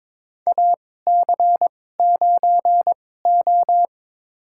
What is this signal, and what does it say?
Morse code, heard as a keyed tone, about 22 wpm, spaced as for 16 wpm, 700 Hz. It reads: AC9O
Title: USA Call Signs - 4 Characters - 2x Character Spacing - 22wpm